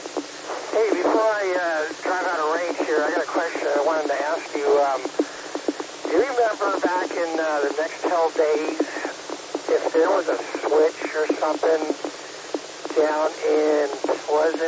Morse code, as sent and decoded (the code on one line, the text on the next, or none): none